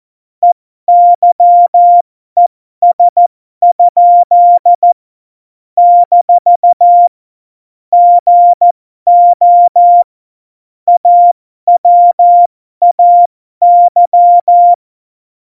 Morse code, heard as a keyed tone, about 14 wpm, spaced as for 12 wpm, 700 Hz. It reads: EYES? - GO AWAY
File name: Morsekurs / Witze